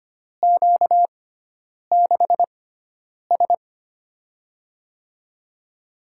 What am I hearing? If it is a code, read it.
Morse code, heard as a keyed tone, about 25 wpm, spaced as for 9 wpm, 700 Hz. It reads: Q6S
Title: USA Call Signs - 4 Characters - 6x Character Spacing - 25wpm